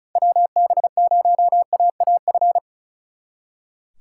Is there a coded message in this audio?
Yes: Morse code from 0.2 to 2.6 s, heard as a keyed tone, about 44 wpm, 700 Hz. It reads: WB0AAF